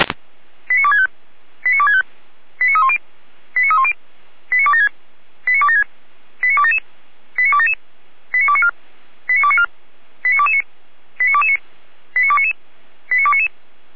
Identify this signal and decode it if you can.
Five-tone selective call (ZVEI), heard as quick runs of five tones, tones of about 70 ms. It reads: 89265 89265 89310 89310 89276 89276 89277 89277 89294 89294 89229 89229 89299 89299